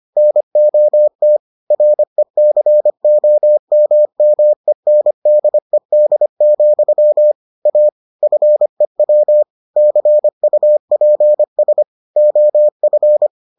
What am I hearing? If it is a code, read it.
Morse code, heard as a keyed tone, about 25 wpm, 600 Hz. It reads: NOT RECOMMENDED, A FEW CUPS OF